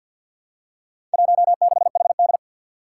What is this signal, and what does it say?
Morse code, heard as a keyed tone, about 51 wpm, 700 Hz. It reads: 16HD
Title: Four Challenging Letter-Number Combinations 50wpm